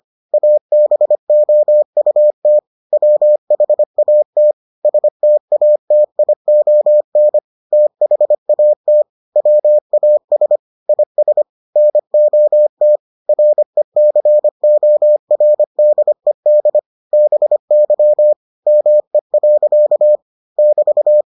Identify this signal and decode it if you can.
Morse code, heard as a keyed tone, about 25 wpm, 600 Hz. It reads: ABOUT WHAT STATION THAT WAS IS NOT RECORDED BY ME. <BT>